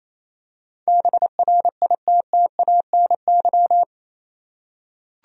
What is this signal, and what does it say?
Morse code, heard as a keyed tone, about 28 wpm, 700 Hz. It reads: BRITTANY
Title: USA Common Names - 28wpm